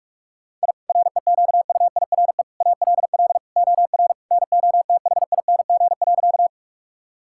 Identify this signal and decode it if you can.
Morse code, heard as a keyed tone, about 46 wpm, 700 Hz. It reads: I REQUIRE ALL OR NOTHING.
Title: Sentences from Top 1000 Words - Review Part 6 - 45wpm